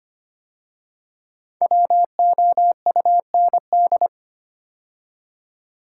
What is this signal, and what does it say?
Morse code, heard as a keyed tone, about 25 wpm, 700 Hz. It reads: WOUND
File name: 5 Letter Words - First 3 Letters Given - 25wpm